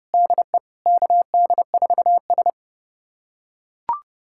Morse code, heard as a keyed tone, about 30 wpm, 700 Hz. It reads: DE KD4S
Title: USA Callsign DE Callsign - 4 Characters - 30wpm